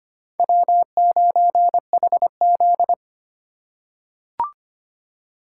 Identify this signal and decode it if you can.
Morse code, heard as a keyed tone, about 25 wpm, 700 Hz. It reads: W9HZ
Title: USA Call Signs - 4 Characters - 3x Character Spacing - 25wpm